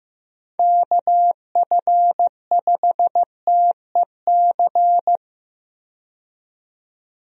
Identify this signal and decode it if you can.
Morse code, heard as a keyed tone, about 15 wpm, 700 Hz. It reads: KF5TEC
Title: USA Call Signs - 6 Characters - 15wpm